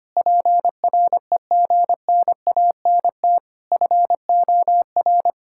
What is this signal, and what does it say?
Morse code, heard as a keyed tone, about 25 wpm, 700 Hz. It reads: PREGNANT FOR
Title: Sentences from Top 1000 Words - Review Part 6 - 25wpm